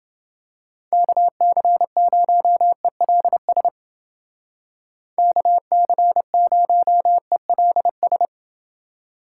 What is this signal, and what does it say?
Morse code, heard as a keyed tone, about 29 wpm, 700 Hz. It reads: KC0ELS KC0ELS